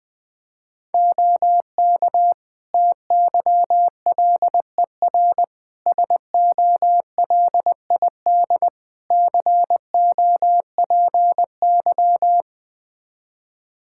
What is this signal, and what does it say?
Morse code, heard as a keyed tone, about 21 wpm, 700 Hz. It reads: OK TYLER SOLID COPY
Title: Practice QSO - Exchange 3 - More Info - 20wpm